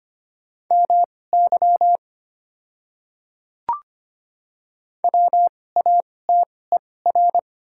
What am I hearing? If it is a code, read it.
Morse code, heard as a keyed tone, about 25 wpm, spaced as for 18 wpm, 700 Hz. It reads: MY WATER
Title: 2x Character Spacing - Top 100 Words - 25wpm